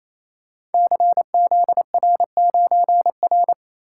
Morse code, heard as a keyed tone, about 29 wpm, 700 Hz. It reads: CZR9R
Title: Code Groups - 5 Letter:Numbers - 28wpm